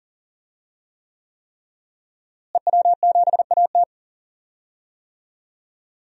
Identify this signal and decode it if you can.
Morse code, heard as a keyed tone, about 39 wpm, 700 Hz. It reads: EW7AT
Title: DXCC-100 Call Signs - 5 Characters - Repeated 3x - 40wpm